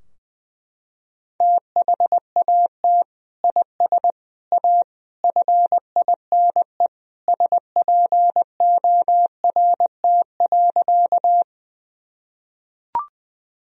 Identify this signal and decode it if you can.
Morse code, heard as a keyed tone, about 19 wpm, 700 Hz. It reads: THAT IS A FINE SPORT.
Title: Sentences from Top 1000 Words - Review Part 4 - 20wpm